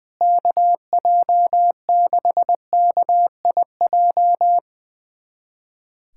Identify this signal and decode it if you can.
Morse code, heard as a keyed tone, about 21 wpm, 700 Hz. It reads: KJ6KIJ
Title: USA Callsign DE Callsign - 6 Characters - 20wpm